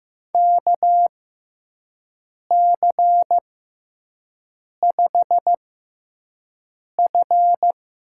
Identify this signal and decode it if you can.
Morse code, heard as a keyed tone, about 16 wpm, spaced as for 5 wpm, 700 Hz. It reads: KC5F
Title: USA Call Signs - 6 Characters - 6x Character Spacing - 15wpm